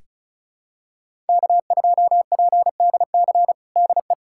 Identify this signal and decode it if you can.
Morse code, heard as a keyed tone, about 36 wpm, 700 Hz. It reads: K2PDC DE
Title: USA Callsign DE Callsign - 5 Characters - 35wpm